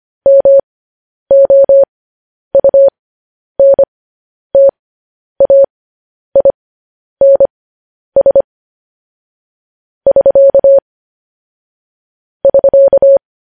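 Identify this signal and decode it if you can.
Morse code, heard as a keyed tone, about 25 wpm, spaced as for 10 wpm, 550 Hz. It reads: MOUNTAINS <SK> <SK>
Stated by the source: Morse code news headlines broadcast at 10 wpm.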